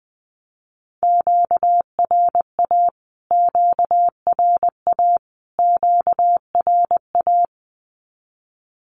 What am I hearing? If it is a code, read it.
Morse code, heard as a keyed tone, about 20 wpm, 700 Hz. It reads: QRA QRA QRA
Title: Q-Codes Repeated 3x 20wpm